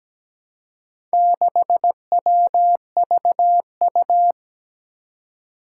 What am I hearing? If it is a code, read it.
Morse code, heard as a keyed tone, about 17 wpm, 700 Hz. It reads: 6WVU
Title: Four Challenging Letter-Number Combinations 17wpm